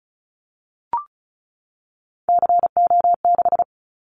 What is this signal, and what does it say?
Morse code, heard as a keyed tone, about 34 wpm, 700 Hz. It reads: CO6
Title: Callsign Prefixes for Top 100 DXCC - 35wpm